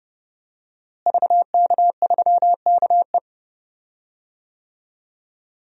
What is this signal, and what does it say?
Morse code, heard as a keyed tone, about 30 wpm, 700 Hz. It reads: VK3KE
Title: DXCC-100 Call Signs - 5 Characters - Repeated 3x - 30wpm